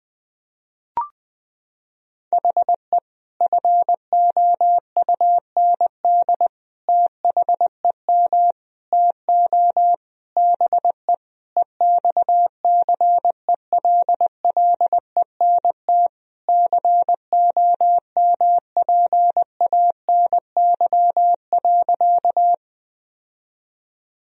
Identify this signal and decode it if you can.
Morse code, heard as a keyed tone, about 21 wpm, 700 Hz. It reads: HE FOUND THEM TO BE EXCELLENT COMPANY.
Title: Sentences from Top 1000 Words - Review Part 3 - 20wpm